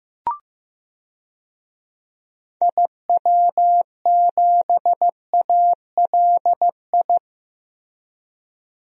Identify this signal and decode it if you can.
Morse code, heard as a keyed tone, about 15 wpm, 700 Hz. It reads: IW7ALI
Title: DXCC-100 Call Signs - 6 Characters - Repeated 3x - 15wpm